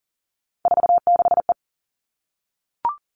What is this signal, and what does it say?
Morse code, heard as a keyed tone, about 40 wpm, 700 Hz. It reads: 46E
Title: All Cut Numbers - 3-Digit 40wpm